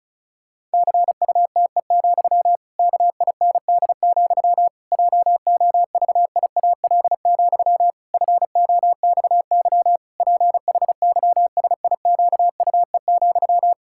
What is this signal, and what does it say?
Morse code, heard as a keyed tone, about 36 wpm, 700 Hz. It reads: CUTE, KIND, JOVIAL, FOXY PHYSIQUE,
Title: Pangram Challenge v2 - 35wpm